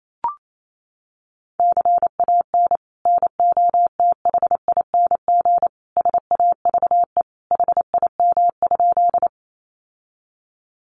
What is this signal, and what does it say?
Morse code, heard as a keyed tone, about 28 wpm, 700 Hz. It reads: CAN NOTHING SAVE HIM?
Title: Sentences from Top 1100 Words 28wpm